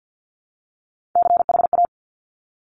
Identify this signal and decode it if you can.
Morse code, heard as a keyed tone, about 50 wpm, 700 Hz. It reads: CHA
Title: Top Three Letter Combinations in Words 50wpm